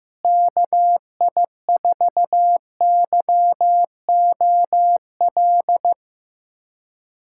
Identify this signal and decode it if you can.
Morse code, heard as a keyed tone, about 15 wpm, 700 Hz. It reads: KI4YOL
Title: USA Callsign DE Callsign - 6 Characters - 15wpm